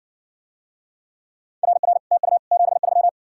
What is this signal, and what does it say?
Morse code, heard as a keyed tone, about 60 wpm, 700 Hz. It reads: RR TU 73